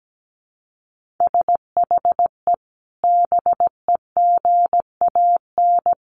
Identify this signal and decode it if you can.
Morse code, heard as a keyed tone, about 17 wpm, 700 Hz. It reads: SHE BEGAN
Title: Sentences from Top 1200 Words 17wpm